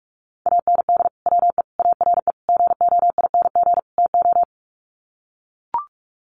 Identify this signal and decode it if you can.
Morse code, heard as a keyed tone, about 44 wpm, 700 Hz. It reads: AND WE ARE GOING TO
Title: Sets of 5 Words Encore2 45wpm